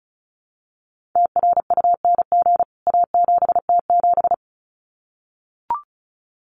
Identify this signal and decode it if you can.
Morse code, heard as a keyed tone, about 34 wpm, 700 Hz. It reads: TRUNG A7T7
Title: CWT Exchange 35wpm